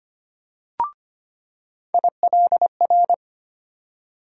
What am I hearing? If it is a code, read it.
Morse code, heard as a keyed tone, about 25 wpm, 700 Hz. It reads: ILR
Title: USA Call Sign Postfixes - 3 Letters 25wpm